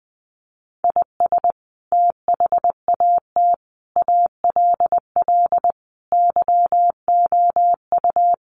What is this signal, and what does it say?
Morse code, heard as a keyed tone, about 20 wpm, 700 Hz. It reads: IS THAT ALL YOU
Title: Sentences from Top 200 Words 20wpm